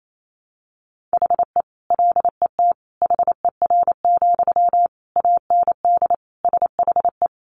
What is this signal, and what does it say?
Morse code, heard as a keyed tone, about 28 wpm, 700 Hz. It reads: HE LET HER, AND SHE
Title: Sentences from Top 1500 Words 28wpm